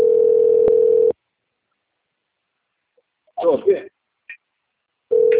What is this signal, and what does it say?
Ringback tone (North America), 440 + 480 Hz, with speech.
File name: out- - - - - .16989